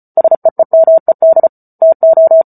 Morse code, heard as a keyed tone, about 34 wpm, 650 Hz. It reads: SEEMED TO